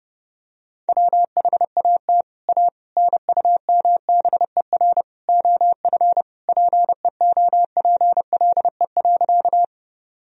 Morse code, heard as a keyed tone, about 30 wpm, 700 Hz. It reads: WHAT A NUMBER OF PEOPLE.
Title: Sentences from Top 1000 Words - Review Part 1 - 30wpm